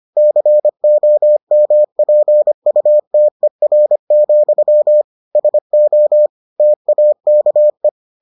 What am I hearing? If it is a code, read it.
Morse code, heard as a keyed tone, about 25 wpm, 600 Hz. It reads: COMPUTER, SO TAKE